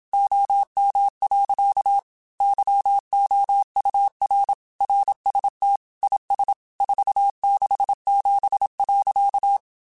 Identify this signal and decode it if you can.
Morse code, heard as a keyed tone, about 27 wpm, 800 Hz. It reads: OM. YOUR RST IS 467.
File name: QSOs / QSOs-mp3s / kc7sac@25